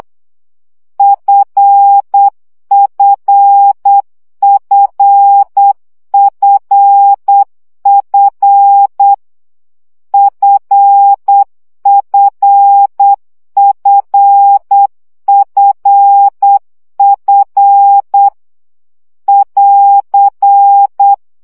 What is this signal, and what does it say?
Morse code, heard as a keyed tone, about 8 wpm, 800 Hz. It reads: FFFFF FFFFF <AR>
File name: CW gyakorlas f 7 baud